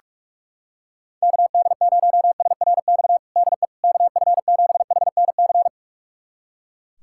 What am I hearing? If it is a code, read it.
Morse code, heard as a keyed tone, about 41 wpm, 690 Hz. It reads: KD0SRX DE KF7HNC
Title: USA Callsign DE Callsign - 6 Characters - 45wpm